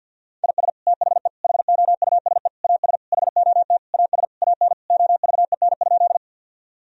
Morse code, heard as a keyed tone, about 50 wpm, 700 Hz. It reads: IS THE HOUSE AS HOT AS AN OVEN?
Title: Sentences from Top 1000 Words - Review Part 4 - 50wpm